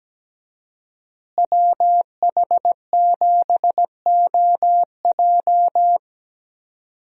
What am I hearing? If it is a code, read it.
Morse code, heard as a keyed tone, about 17 wpm, 700 Hz. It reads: WH7OJ
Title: USA Call Signs - 5 Characters - Repeated 3x - 17wpm